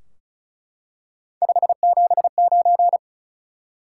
Morse code, heard as a keyed tone, about 36 wpm, 700 Hz. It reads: 579
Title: Top 180 QSO Elements 35wpm